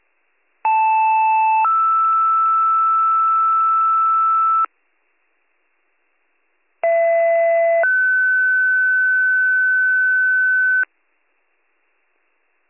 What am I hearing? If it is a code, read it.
Two-tone page, heard as a one-second tone, then a three-second tone, about 880 then 1320 Hz, about 670 then 1500 Hz. It is heard from 0.6 to 10.8 s.